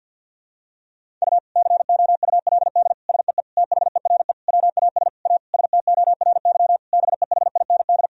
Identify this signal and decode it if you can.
Morse code, heard as a keyed tone, about 52 wpm, 700 Hz. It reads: U COULD SEE THERE WAS A STORY BEHIND
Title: Sentences from Top 500 Words and Abbreviations 50wpm